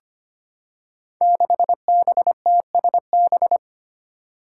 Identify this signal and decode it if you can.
Morse code, heard as a keyed tone, about 25 wpm, 700 Hz. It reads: 6BTSB
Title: Five Challenging Letter-Number Combinations 25wpm